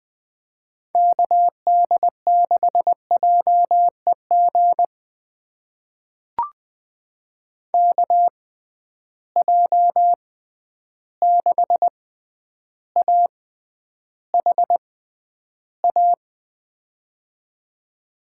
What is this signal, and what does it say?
Morse code, heard as a keyed tone, about 20 wpm, spaced as for 10 wpm, 700 Hz. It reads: KD6JEG KJ6AHA
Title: USA Call Signs - 6 Characters - 6x Character Spacing - 20wpm